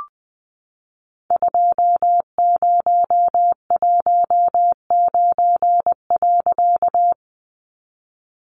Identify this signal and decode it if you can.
Morse code, heard as a keyed tone, about 20 wpm, 700 Hz. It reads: 2019.
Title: A Sound Of Thunder 20wpm